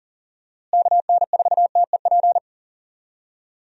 Morse code, heard as a keyed tone, about 40 wpm, 700 Hz. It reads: KN4TEP
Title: USA Call Signs - 6 Characters - Repeated 3x - 40wpm